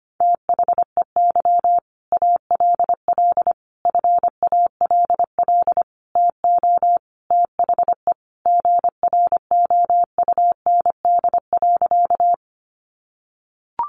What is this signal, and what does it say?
Morse code, heard as a keyed tone, about 25 wpm, 700 Hz. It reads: THEY ALL FALL TO THE GROUND.
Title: Sentences from Top 500 Words 25wpm